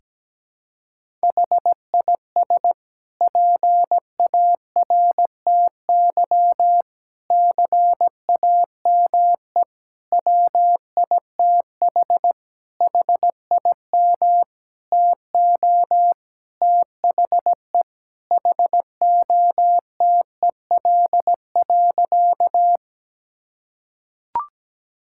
Morse code, heard as a keyed tone, about 17 wpm, 700 Hz. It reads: HIS PARTY CAME WITH HIM TO THE HOTEL.
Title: Sentences from Top 1600 Words 17wpm